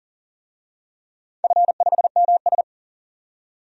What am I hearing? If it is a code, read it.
Morse code, heard as a keyed tone, about 40 wpm, 700 Hz. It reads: F5MS
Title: DXCC-100 Call Signs - 4 Characters - 40wpm